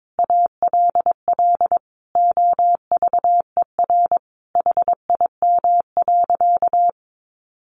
Morse code, heard as a keyed tone, about 22 wpm, 700 Hz. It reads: ALL OVER HIM.
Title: Additional Easy Phrases 22wpm